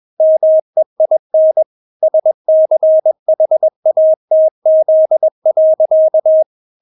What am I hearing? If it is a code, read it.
Morse code, heard as a keyed tone, about 21 wpm, 630 Hz. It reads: MEIN SCHATZ.
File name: Witz2 21wpm